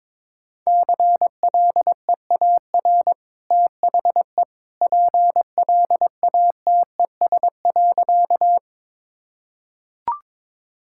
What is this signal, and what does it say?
Morse code, heard as a keyed tone, about 22 wpm, 700 Hz. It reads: CLEAR THE PLATES.
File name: practice / 22wpm / Easy Sentences with Hints - 22wpm